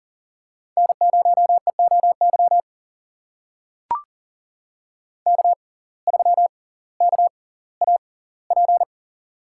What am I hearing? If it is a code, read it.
Morse code, heard as a keyed tone, about 40 wpm, spaced as for 20 wpm, 700 Hz. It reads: N0EOY K3KAP
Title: USA Call Signs - 5 Characters - 6x Character Spacing - 40wpm